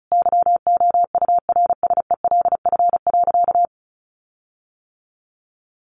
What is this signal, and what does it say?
Morse code, heard as a keyed tone, about 34 wpm, 700 Hz. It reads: YOURSELF.
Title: Sentences from Top 1400 Words 35wpm